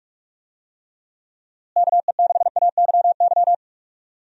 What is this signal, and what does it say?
Morse code, heard as a keyed tone, about 46 wpm, 700 Hz. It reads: KE6AYY
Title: USA Call Signs - 6 Characters - Repeated 2x - 45wpm